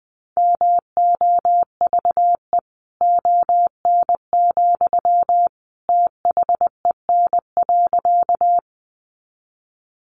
Morse code, heard as a keyed tone, about 20 wpm, 700 Hz. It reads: MOVE ON, THEN.
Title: Sentences from Top 200 Words 20wpm